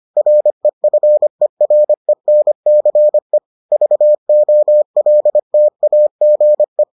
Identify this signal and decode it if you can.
Morse code, heard as a keyed tone, about 25 wpm, 600 Hz. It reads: REFERENCE VOLTAGE